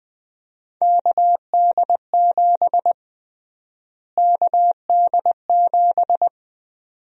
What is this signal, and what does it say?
Morse code, heard as a keyed tone, about 20 wpm, 700 Hz. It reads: KD7 KD7
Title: USA Call Sign Prefixes - Repeated 2x - 20wpm